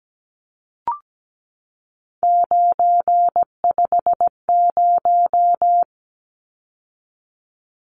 Morse code, heard as a keyed tone, about 17 wpm, 700 Hz. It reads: 950
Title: Numbers - 3-Digit 17wpm